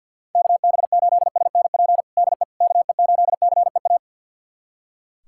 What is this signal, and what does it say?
Morse code, heard as a keyed tone, about 46 wpm, 700 Hz. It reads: KB8SNP DE KE8CEA
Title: USA Callsign DE Callsign - 6 Characters - 50wpm